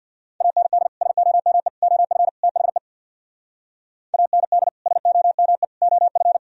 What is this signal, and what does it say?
Morse code, heard as a keyed tone, about 61 wpm, 700 Hz. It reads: AND SOME OF THE AND SOME OF
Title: Speed-Racing - Sets of 4 Words 50wpm